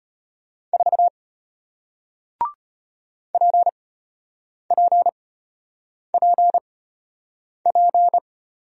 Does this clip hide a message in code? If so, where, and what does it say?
Yes: Morse code from 0.7 to 8.2 s, heard as a keyed tone, about 33 wpm, spaced as for 8 wpm, 700 Hz. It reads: 4 PPPP